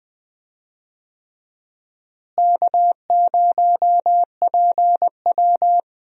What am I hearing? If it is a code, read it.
Morse code, heard as a keyed tone, about 19 wpm, 700 Hz. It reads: K0PW
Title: USA Call Signs - 4 Characters - Repeated 3x - 20wpm